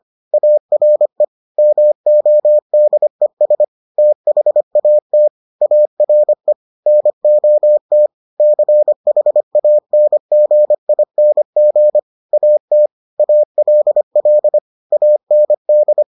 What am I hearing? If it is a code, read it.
Morse code, heard as a keyed tone, about 25 wpm, 600 Hz. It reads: ARE MODES THAT ARE NOT CHANGING AT ALL AND